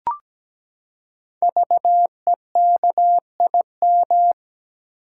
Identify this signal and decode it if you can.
Morse code, heard as a keyed tone, about 17 wpm, 700 Hz. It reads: VEKIM
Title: Code Groups - 5 Letters - 17wpm